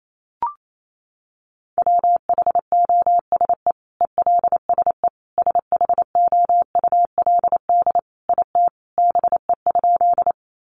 Morse code, heard as a keyed tone, about 28 wpm, 700 Hz. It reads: WHOSE ELSE SHOULD IT BE?
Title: Sentences from Top 1200 Words 28wpm